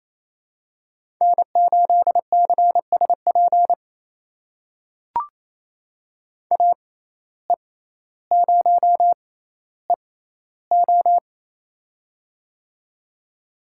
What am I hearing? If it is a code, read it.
Morse code, heard as a keyed tone, about 28 wpm, spaced as for 14 wpm, 700 Hz. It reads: N8CSP AE0EO
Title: USA Call Signs - 5 Characters - 6x Character Spacing - 28wpm